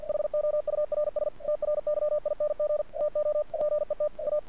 Morse code, heard as a keyed tone, about 50 wpm, 600 Hz. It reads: HOURS TRYING TO GET A
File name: archived / CRSnet